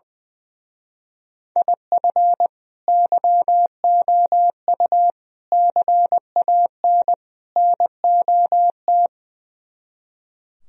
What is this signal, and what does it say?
Morse code, heard as a keyed tone, about 20 wpm, 700 Hz. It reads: IF YOU CAN NOT
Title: Speed-Racing - Sets of 4 Words 20wpm